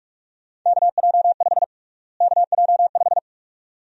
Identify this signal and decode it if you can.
Morse code, heard as a keyed tone, about 42 wpm, 700 Hz. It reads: KJ5 KJ5
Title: USA Call Sign Prefixes - Repeated 2x - 45wpm